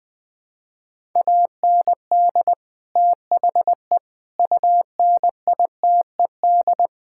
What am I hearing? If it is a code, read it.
Morse code, heard as a keyed tone, about 20 wpm, 700 Hz. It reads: AND THE UNITED
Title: Sentences with 4-Word Phrases - 20wpm